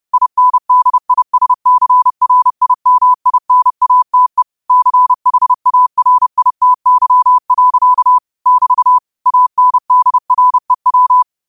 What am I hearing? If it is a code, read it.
Morse code, heard as a keyed tone, about 30 wpm, 1000 Hz. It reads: INDISCRIMINATE CHARITY. <BT> ANDREW
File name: qotdfiles / 30wpm